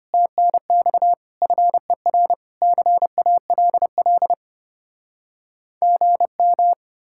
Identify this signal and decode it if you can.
Morse code, heard as a keyed tone, about 33 wpm, 700 Hz. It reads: TNX FER CALL GM